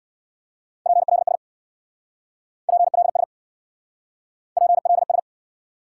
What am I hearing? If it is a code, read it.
Morse code, heard as a keyed tone, about 70 wpm, 700 Hz. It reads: PLS PLS PLS